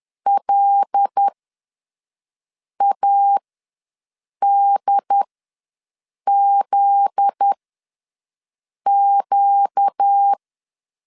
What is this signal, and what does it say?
Morse code, heard as a keyed tone, about 10 wpm, spaced as for 5 wpm, 800 Hz. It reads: LADZQ
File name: LEZIONI / 7 GF9Z / 7 GF9Z ws 10wpm 3919